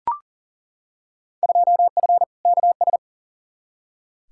Morse code, heard as a keyed tone, about 40 wpm, 700 Hz. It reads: 2F KS
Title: Field Day Exchanges - 40wpm